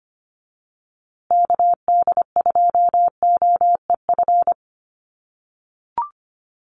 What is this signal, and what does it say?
Morse code, heard as a keyed tone, about 25 wpm, 700 Hz. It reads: KD2OEF
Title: USA Call Signs Repeated 3x 25wpm